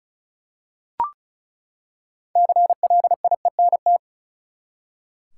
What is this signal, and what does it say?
Morse code, heard as a keyed tone, about 34 wpm, 700 Hz. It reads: CLIENT
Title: Six Letter Words - 35wpm